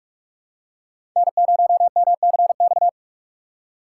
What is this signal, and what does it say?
Morse code, heard as a keyed tone, about 47 wpm, 690 Hz. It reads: N0MCX
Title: USA Call Signs - 5 Characters - 2x Character Spacing - 45wpm